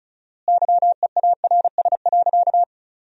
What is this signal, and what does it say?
Morse code, heard as a keyed tone, about 34 wpm, 700 Hz. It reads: YEARS.
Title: 4x Word-Spacing - Sentences from Top 200 Words - 35wpm